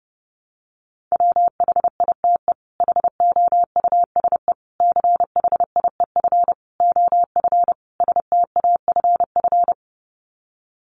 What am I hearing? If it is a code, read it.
Morse code, heard as a keyed tone, about 30 wpm, 700 Hz. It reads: WHITE HOUSE CHIEF OF STAFF
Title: Sets of 5 Words Encore 30wpm